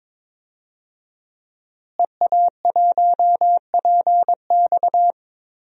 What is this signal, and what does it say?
Morse code, heard as a keyed tone, about 21 wpm, 700 Hz. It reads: EA1PX
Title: DXCC-100 Call Signs - 5 Characters - Repeated 2x - 22wpm